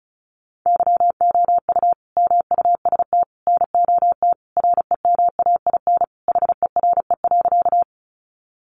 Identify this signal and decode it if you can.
Morse code, heard as a keyed tone, about 34 wpm, 700 Hz. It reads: YOU MUST NOT REMAIN HERE.
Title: Sentences from Top 1200 Words 35wpm